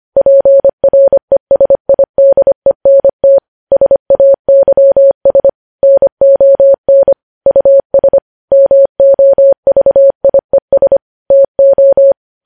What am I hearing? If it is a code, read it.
Morse code, heard as a keyed tone, about 25 wpm, 550 Hz. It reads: PRESIDENT SAYS NON US MOVIES TO
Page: news-headlines-in-morse-code-at-25-wpm